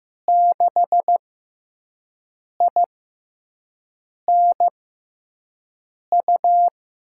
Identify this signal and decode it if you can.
Morse code, heard as a keyed tone, about 15 wpm, spaced as for 5 wpm, 700 Hz. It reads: 6INU
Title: USA Call Signs - 6x Character Spacing - 15wpm